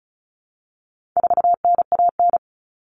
Morse code, heard as a keyed tone, about 34 wpm, 700 Hz. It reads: 4NAN
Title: Common Cut Numbers - 4-Digit 35wpm